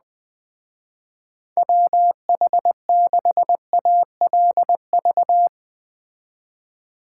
Morse code, heard as a keyed tone, about 20 wpm, 700 Hz. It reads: WH6ALV